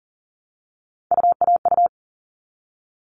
Morse code, heard as a keyed tone, about 40 wpm, 700 Hz. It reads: UAU
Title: All Cut Numbers - 3-Digit 40wpm